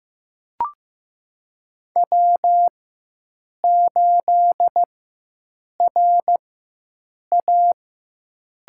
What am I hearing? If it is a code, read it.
Morse code, heard as a keyed tone, about 15 wpm, spaced as for 7 wpm, 700 Hz. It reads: W8RA
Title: USA Call Signs - 4x Character Spacing - 15wpm